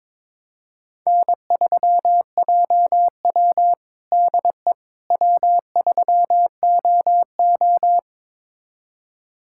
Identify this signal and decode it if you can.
Morse code, heard as a keyed tone, about 21 wpm, 700 Hz. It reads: N3JW DE W3OO
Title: USA Callsign DE Callsign - 4 Characters - 22wpm